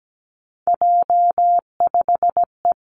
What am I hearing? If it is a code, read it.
Morse code, heard as a keyed tone, about 17 wpm, 700 Hz. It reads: J5E
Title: Three Challenging Letter-Number Combinations 17wpm